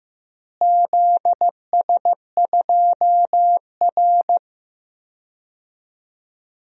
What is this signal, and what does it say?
Morse code, heard as a keyed tone, about 15 wpm, 700 Hz. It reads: ZS2R
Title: DXCC-100 Call Signs - 4 Characters - Repeated 3x - 15wpm